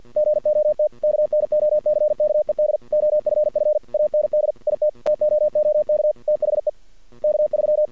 Morse code, heard as a keyed tone, about 48 wpm, 600 Hz. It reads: GOT KNOCKED OFF AND IT TOOK THE K3
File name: archived / CRSnet